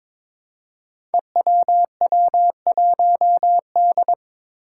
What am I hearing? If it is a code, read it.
Morse code, heard as a keyed tone, about 21 wpm, 700 Hz. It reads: EWW1D
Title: Five Challenging Letter-Number Combinations 22wpm